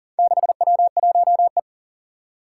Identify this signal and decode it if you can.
Morse code, heard as a keyed tone, about 41 wpm, 700 Hz. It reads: 6W1E
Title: Four Challenging Letter-Number Combinations 40wpm